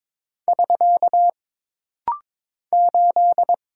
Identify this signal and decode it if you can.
Morse code, heard as a keyed tone, about 22 wpm, 700 Hz. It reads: <SK> 8
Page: This rapid-fire timing is geared to push your ICR (Instant Character Recognition) skills.